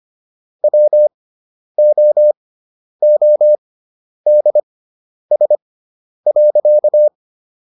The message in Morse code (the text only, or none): WOODS.